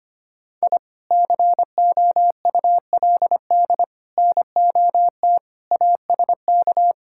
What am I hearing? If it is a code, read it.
Morse code, heard as a keyed tone, about 25 wpm, 700 Hz. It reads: I COULD NOT ASK
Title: Sentences from Top 1000 Words - Review Part 5 - 25wpm